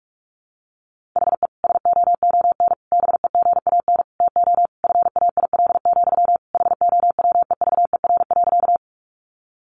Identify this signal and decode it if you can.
Morse code, heard as a keyed tone, about 44 wpm, 700 Hz. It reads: HE SOON BEGAN TO FAIL, HOWEVER.
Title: Sentences from Top 1100 Words 45wpm